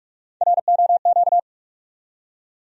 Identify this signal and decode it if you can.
Morse code, heard as a keyed tone, about 46 wpm, 700 Hz. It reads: ROQ
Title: USA Call Sign Postfixes - 3 Letters 45wpm